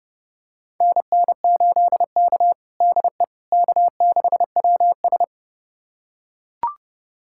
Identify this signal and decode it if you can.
Morse code, heard as a keyed tone, about 30 wpm, 700 Hz. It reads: NN8K DE K6WS
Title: USA Callsign DE Callsign - 4 Characters - 30wpm